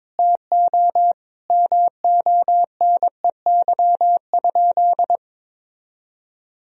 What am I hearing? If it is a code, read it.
Morse code, heard as a keyed tone, about 22 wpm, 700 Hz. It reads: TO MONEY?
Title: Sentences from Top 1000 Words - Review Part 4 - 22wpm